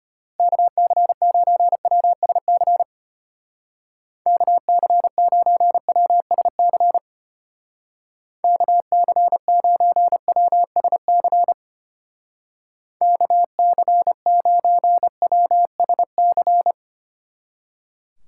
Morse code, heard as a keyed tone, about 32 wpm, 700 Hz. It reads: KC9WSC KC9WSC KC9WSC KC9WSC